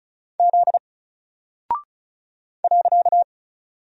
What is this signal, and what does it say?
Morse code, heard as a keyed tone, about 35 wpm, 700 Hz. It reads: Z .
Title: Single Character (Letter:Number:Punctuation:Prosign) 35wpm